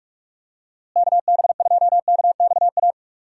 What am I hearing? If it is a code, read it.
Morse code, heard as a keyed tone, about 46 wpm, 700 Hz. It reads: KB2KXA
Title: USA Call Signs - 6 Characters - Repeated 3x - 45wpm